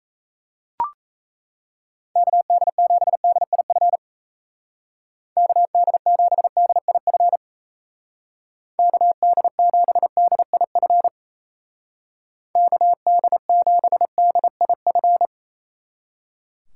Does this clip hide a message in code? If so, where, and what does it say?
Yes: Morse code from 2.2 to 15.3 s, heard as a keyed tone, about 37 wpm, 700 Hz. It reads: KD7DIF KD7DIF KD7DIF KD7DIF